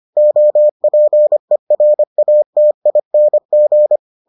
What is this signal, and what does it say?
Morse code, heard as a keyed tone, about 25 wpm, 600 Hz. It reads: OPERATING